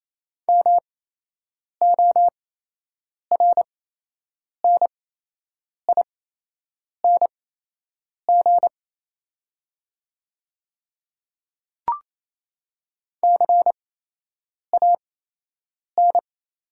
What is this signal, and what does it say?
Morse code, heard as a keyed tone, about 28 wpm, spaced as for 8 wpm, 700 Hz. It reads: MORNING CAN
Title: 8x Character Spacing - Top 500 Words - 28wpm